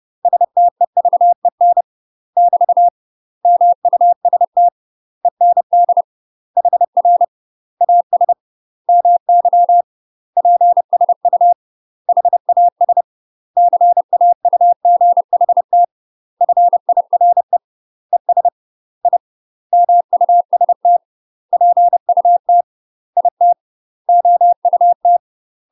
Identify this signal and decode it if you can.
Morse code, heard as a keyed tone, about 30 wpm, 700 Hz. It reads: STEVEN <BT> MUST END HR AS MY PSU HAS CAUGHT FIRE ES I MUST PUT IT OUT
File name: UK MASTER QSOs TRAINING / 30WPM / UK MASTER QSOs 0011